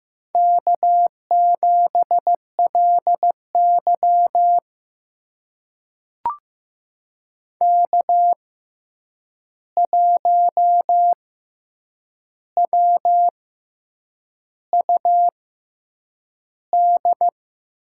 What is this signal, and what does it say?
Morse code, heard as a keyed tone, about 15 wpm, spaced as for 7 wpm, 700 Hz. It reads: K7LY K1WUD
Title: USA Call Signs - 6x Character Spacing - 15wpm